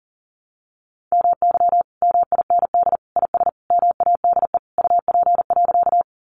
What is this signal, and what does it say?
Morse code, heard as a keyed tone, about 37 wpm, 700 Hz. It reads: MY MIND IS MADE UP.
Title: Sentences from Top 500 Words 40wpm